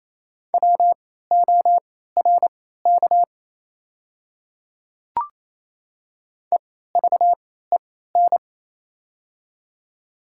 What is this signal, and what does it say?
Morse code, heard as a keyed tone, about 28 wpm, spaced as for 16 wpm, 700 Hz. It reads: WORK EVEN
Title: 3x Character Spacing - Top 200 Words - 28wpm